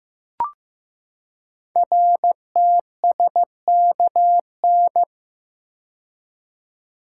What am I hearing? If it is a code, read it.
Morse code, heard as a keyed tone, about 15 wpm, 700 Hz. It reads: RTSKN
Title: Code Groups - 5 Letters - 15wpm